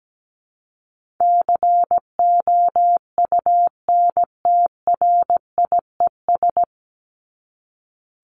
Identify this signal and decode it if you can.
Morse code, heard as a keyed tone, about 17 wpm, 700 Hz. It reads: COUNTRIES
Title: ES:S Suffix - Top 100 Words with Base Words 17wpm